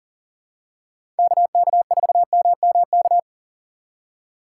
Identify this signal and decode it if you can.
Morse code, heard as a keyed tone, about 40 wpm, 700 Hz. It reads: KK4MMK
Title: USA Call Signs - 6 Characters - 2x Character Spacing - 40wpm